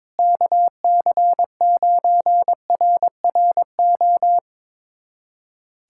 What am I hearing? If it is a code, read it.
Morse code, heard as a keyed tone, about 22 wpm, 700 Hz. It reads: KC9RRO
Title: USA Callsign DE Callsign - 6 Characters - 22wpm